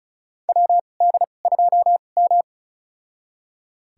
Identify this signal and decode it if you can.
Morse code, heard as a keyed tone, about 36 wpm, spaced as for 25 wpm, 700 Hz. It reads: WD2M
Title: USA Call Signs - 4 Characters - 2x Character Spacing - 35wpm